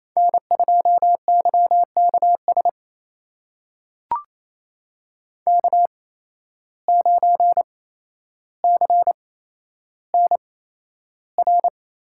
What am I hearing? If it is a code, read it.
Morse code, heard as a keyed tone, about 28 wpm, spaced as for 12 wpm, 700 Hz. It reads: N2YKS K9CNR
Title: USA Call Signs - 5 Characters - 8x Character Spacing - 28wpm